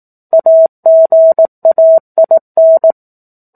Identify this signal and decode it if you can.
Morse code, heard as a keyed tone, about 18 wpm, 650 Hz. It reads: AGAIN
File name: 04-Chapter-04 18 WPM